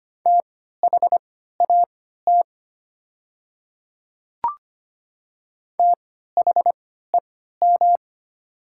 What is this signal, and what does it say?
Morse code, heard as a keyed tone, about 25 wpm, spaced as for 14 wpm, 700 Hz. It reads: THAT THEM
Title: 3x Character Spacing - Top 100 Words - 25wpm